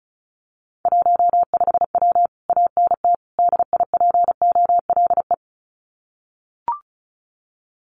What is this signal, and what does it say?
Morse code, heard as a keyed tone, about 36 wpm, 700 Hz. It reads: 15W ANT DIPOLE
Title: Common QSO Phrases Encore 35wpm